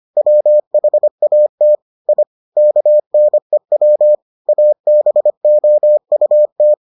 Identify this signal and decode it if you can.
Morse code, heard as a keyed tone, about 25 wpm, 600 Hz. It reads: WHAT I KNEW ABOUT